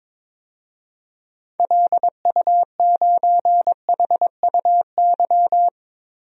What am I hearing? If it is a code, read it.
Morse code, heard as a keyed tone, about 22 wpm, 700 Hz. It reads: LU9HUY
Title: DXCC-100 Call Signs - 6 Characters - 22wpm